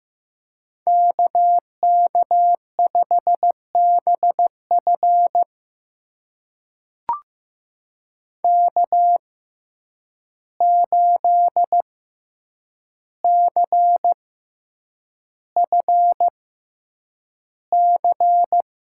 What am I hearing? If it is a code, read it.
Morse code, heard as a keyed tone, about 15 wpm, spaced as for 8 wpm, 700 Hz. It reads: KK5BF K8CFC
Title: USA Call Signs - 6x Character Spacing - 15wpm